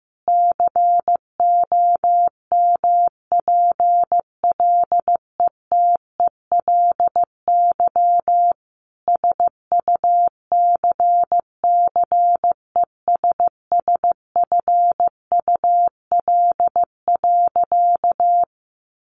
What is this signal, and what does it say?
Morse code, heard as a keyed tone, about 15 wpm, 700 Hz. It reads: COMPLETELY SUCCESSFUL.
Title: Sentences from Top 1100 Words 15wpm